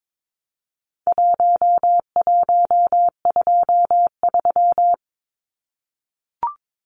Morse code, heard as a keyed tone, about 22 wpm, 700 Hz. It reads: 1123
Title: Numbers - 4-Digit 22wpm